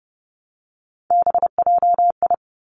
Morse code, heard as a keyed tone, about 30 wpm, 700 Hz. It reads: BJI
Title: Three Challenging Letter-Number Combinations 30wpm